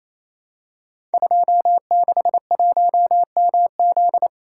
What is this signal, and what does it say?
Morse code, heard as a keyed tone, about 28 wpm, 700 Hz. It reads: 261MZ
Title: Code Groups - 5 Letter:Numbers - 28wpm